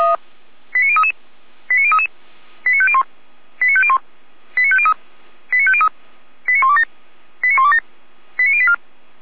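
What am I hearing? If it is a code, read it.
Five-tone selective call (ZVEI), heard as quick runs of five tones, tones of about 70 ms. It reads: 89033 89033 89501 89501 89503 89503 89127 89127 89084
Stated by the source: ZVEI five-tone selective call received on an SDR.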